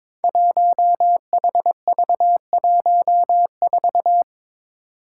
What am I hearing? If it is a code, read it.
Morse code, heard as a keyed tone, about 22 wpm, 700 Hz. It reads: 1HV14
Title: Five Challenging Letter-Number Combinations 22wpm